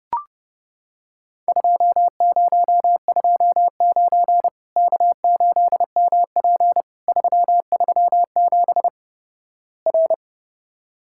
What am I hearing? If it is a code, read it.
Morse code, heard as a keyed tone, about 32 wpm, 700 Hz. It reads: 2029 K8MP 337 R
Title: Worked All Europe DX Contest - QTC Exchange - 30wpm